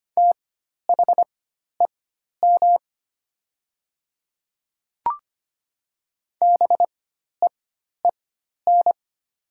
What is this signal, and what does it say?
Morse code, heard as a keyed tone, about 25 wpm, spaced as for 12 wpm, 690 Hz. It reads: THEM BEEN
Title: 4x Character Spacing - Top 200 Words - 25wpm